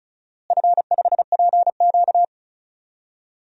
Morse code, heard as a keyed tone, about 34 wpm, 700 Hz. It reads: F5PQ
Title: DXCC-100 Call Signs - 4 Characters - Repeated 3x - 35wpm